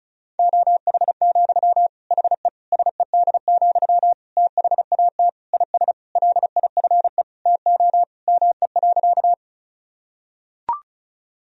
Morse code, heard as a keyed tone, about 36 wpm, 700 Hz. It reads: OH, HE SED, THAT IS LIFE TO ME.
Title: Sentences from Top 500 Words and Abbreviations 35wpm